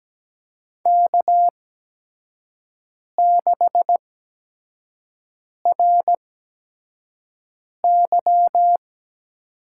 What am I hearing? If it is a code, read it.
Morse code, heard as a keyed tone, about 17 wpm, spaced as for 5 wpm, 700 Hz. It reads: K6RY
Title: USA Call Signs - 5 Characters - 8x Character Spacing - 17wpm